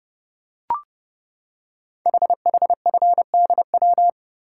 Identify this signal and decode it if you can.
Morse code, heard as a keyed tone, about 30 wpm, 700 Hz. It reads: HHFDW
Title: Code Groups - 5 Letters - 30wpm